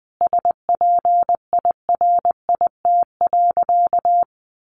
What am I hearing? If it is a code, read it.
Morse code, heard as a keyed tone, about 20 wpm, 700 Hz. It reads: SPIRIT.
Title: Sentences from Top 1400 Words 20wpm